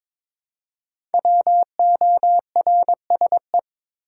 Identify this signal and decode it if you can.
Morse code, heard as a keyed tone, about 22 wpm, 700 Hz. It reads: WORSE
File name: 5 Letter Words - First 3 Letters Given - 22wpm